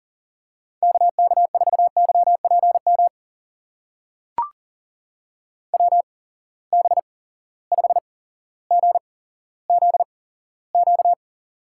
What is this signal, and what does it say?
Morse code, heard as a keyed tone, about 39 wpm, spaced as for 17 wpm, 700 Hz. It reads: KK4YPM WB5GZQ
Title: USA Call Signs - 6 Characters - 8x Character Spacing - 40wpm